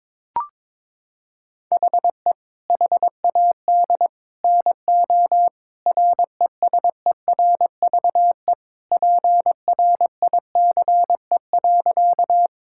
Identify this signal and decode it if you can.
Morse code, heard as a keyed tone, about 23 wpm, 700 Hz. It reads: HE HAD NO RESERVE PRICE.
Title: Sentences from Top 1000 Words - Review Part 6 - 22wpm